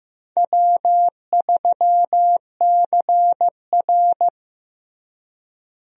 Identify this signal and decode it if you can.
Morse code, heard as a keyed tone, about 15 wpm, 700 Hz. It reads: W3CR
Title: USA Call Signs - 4 Characters - Repeated 2x - 15wpm